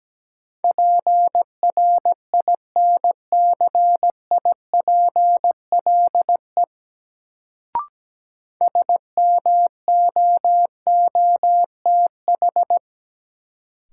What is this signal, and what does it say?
Morse code, heard as a keyed tone, about 17 wpm, 700 Hz. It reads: PRINCIPLE SMOOTH
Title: Sentences from Top 1400 Words - New Word Drill Alt - 17wpm